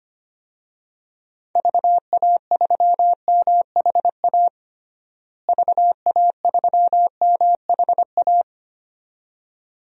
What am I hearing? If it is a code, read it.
Morse code, heard as a keyed tone, about 25 wpm, 700 Hz. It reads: VA3MHA VA3MHA
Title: DXCC-100 Call Signs - 6 Characters - Repeated 2x - 25wpm